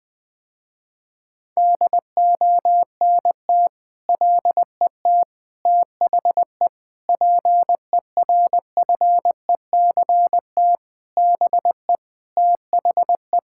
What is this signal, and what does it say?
Morse code, heard as a keyed tone, about 20 wpm, 700 Hz. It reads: DONT LET THE PERFECT BE THE
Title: Sentences with 9-Word Phrases - Spoken Beginning - 20wpm